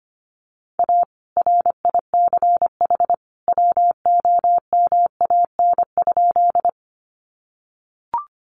Morse code, heard as a keyed tone, about 25 wpm, 700 Hz. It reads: A RICH WOMAN?
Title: Sentences from Top 1100 Words 25wpm